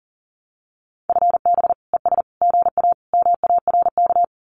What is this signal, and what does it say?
Morse code, heard as a keyed tone, about 40 wpm, 700 Hz. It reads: FB ES GA MARK
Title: Common QSO Phrases Encore 40wpm